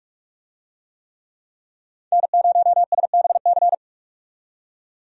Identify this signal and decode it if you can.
Morse code, heard as a keyed tone, about 46 wpm, 700 Hz. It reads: N0SBC
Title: USA Call Signs - 5 Characters - 45wpm